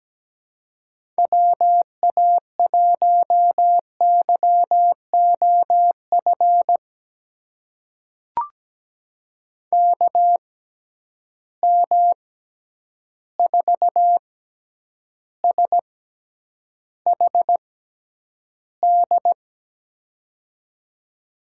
Morse code, heard as a keyed tone, about 17 wpm, spaced as for 9 wpm, 700 Hz. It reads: WA1YOF KM4SHD
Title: USA Call Signs - 6 Characters - 6x Character Spacing - 17wpm